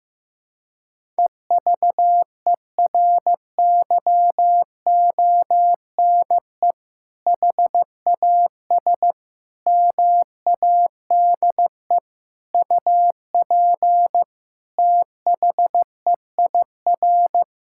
Morse code, heard as a keyed tone, about 15 wpm, 700 Hz. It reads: EVERYONE HAS MADE UP THEIR
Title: Sentences from Top 1000 Words - Review Part 5 - 15wpm